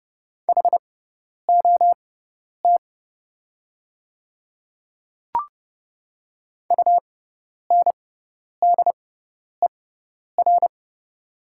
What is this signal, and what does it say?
Morse code, heard as a keyed tone, about 29 wpm, spaced as for 10 wpm, 690 Hz. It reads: HOT UNDER
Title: 6x Character Spacing - Top 200 Words - 30wpm